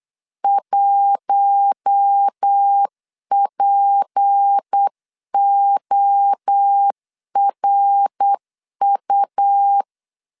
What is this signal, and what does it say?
Morse code, heard as a keyed tone, about 8 wpm, 800 Hz. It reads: 1PORU
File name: CW / DET / 1 6 1500C S C 8wpm 5219